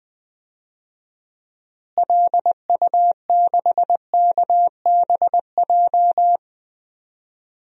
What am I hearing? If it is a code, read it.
Morse code, heard as a keyed tone, about 20 wpm, 700 Hz. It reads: LU6KBJ
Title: DXCC-100 Call Signs - 6 Characters - Repeated 3x - 20wpm